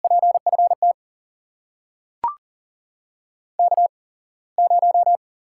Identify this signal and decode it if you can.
Morse code, heard as a keyed tone, about 40 wpm, spaced as for 21 wpm, 700 Hz. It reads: PFT K0
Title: USA Call Signs - 5 Characters - 8x Character Spacing - 40wpm